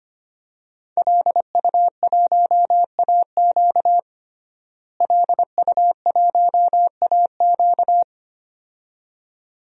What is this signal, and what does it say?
Morse code, heard as a keyed tone, about 25 wpm, 700 Hz. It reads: LU1AQ LU1AQ
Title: DXCC-100 Call Signs - 5 Characters - Repeated 2x - 25wpm